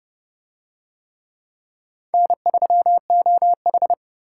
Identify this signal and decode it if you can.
Morse code, heard as a keyed tone, about 29 wpm, 700 Hz. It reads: N3OH